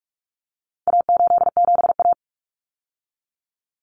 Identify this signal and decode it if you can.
Morse code, heard as a keyed tone, about 44 wpm, 700 Hz. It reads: A87A
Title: Common Cut Numbers - 4-Digit 45wpm